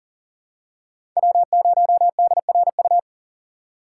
Morse code, heard as a keyed tone, about 37 wpm, 700 Hz. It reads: W0DRU
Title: USA Call Signs - 4x Character Spacing - 40wpm